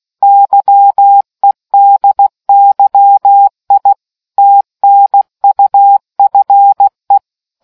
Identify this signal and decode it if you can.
Morse code, heard as a keyed tone, about 16 wpm, 800 Hz. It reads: YEDYI TNUFE